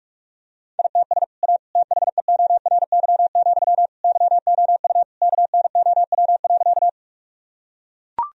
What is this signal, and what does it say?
Morse code, heard as a keyed tone, about 43 wpm, 700 Hz. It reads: ITS A THEORY, YOU KNOW.
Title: Sentences from Top 1000 Words - Review Part 3 - 45wpm